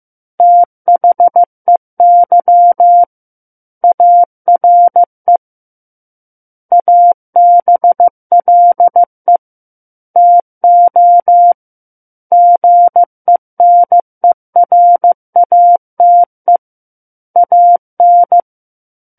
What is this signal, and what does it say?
Morse code, heard as a keyed tone, about 15 wpm, 700 Hz. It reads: THEY ARE ABLE TO GENERATE AN
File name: Book One / 06-Chapter 6 - 15 WPM